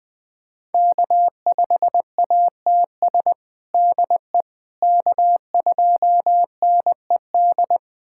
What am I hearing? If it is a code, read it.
Morse code, heard as a keyed tone, about 20 wpm, 700 Hz. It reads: K5ATS DE K2NED
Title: USA Callsign DE Callsign - 5 Characters - 20wpm